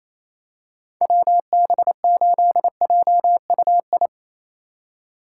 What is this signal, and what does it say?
Morse code, heard as a keyed tone, about 27 wpm, 700 Hz. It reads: WB8JUI
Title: K1USN SST QSO Practice - 28wpm